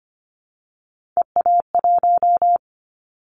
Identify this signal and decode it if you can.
Morse code, heard as a keyed tone, about 25 wpm, 700 Hz. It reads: EA1
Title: Callsign Prefixes for Top 25 DXCC - 25wpm